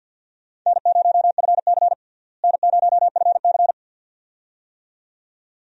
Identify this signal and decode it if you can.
Morse code, heard as a keyed tone, about 51 wpm, 700 Hz. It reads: N0FC N0FC
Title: USA Call Signs - 4 Characters - Repeated 2x - 50wpm